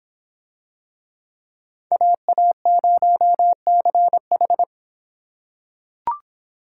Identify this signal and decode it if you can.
Morse code, heard as a keyed tone, about 24 wpm, 700 Hz. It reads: AA0CH